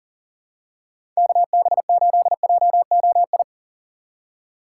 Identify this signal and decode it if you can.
Morse code, heard as a keyed tone, about 39 wpm, 700 Hz. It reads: KB8JOI
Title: USA Call Signs - 6x Character Spacing - 40wpm